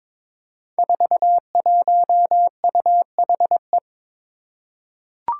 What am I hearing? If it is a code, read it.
Morse code, heard as a keyed tone, about 21 wpm, 700 Hz. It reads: 41UHE
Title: Five Challenging Letter-Number Combinations 22wpm